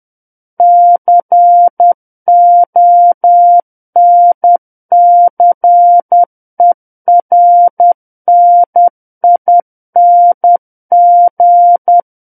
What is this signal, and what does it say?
Morse code, heard as a keyed tone, about 10 wpm, 700 Hz. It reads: CONCERNING
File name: Book One / 01-Chapter 1 - 10 WPM